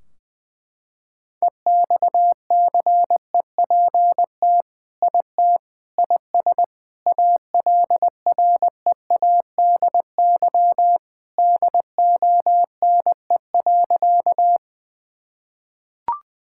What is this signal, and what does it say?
Morse code, heard as a keyed tone, about 20 wpm, 700 Hz. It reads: EXCEPT IT IS ALREADY DONE.
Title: Sentences from Top 1600 Words 20wpm